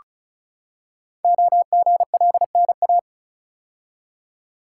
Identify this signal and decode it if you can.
Morse code, heard as a keyed tone, about 36 wpm, 700 Hz. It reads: OGLNA
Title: Code Groups - 5 Letters - 35wpm